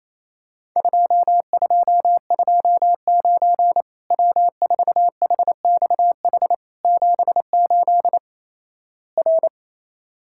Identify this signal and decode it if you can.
Morse code, heard as a keyed tone, about 28 wpm, 700 Hz. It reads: 2229 W4HXH 78 R
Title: Worked All Europe DX Contest - QTC Exchange - 28wpm